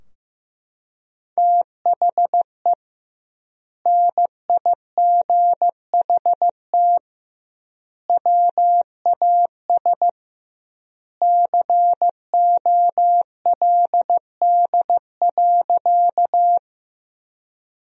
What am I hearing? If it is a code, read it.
Morse code, heard as a keyed tone, about 15 wpm, 700 Hz. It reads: THE NIGHT WAS COLD.
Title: 2x Word-Spacing - Sentences from Top 400 Words - 15wpm